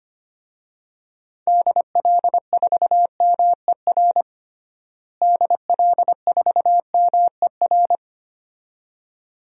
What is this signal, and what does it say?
Morse code, heard as a keyed tone, about 25 wpm, 700 Hz. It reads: DL4MER DL4MER
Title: DXCC-100 Call Signs - 6 Characters - Repeated 2x - 25wpm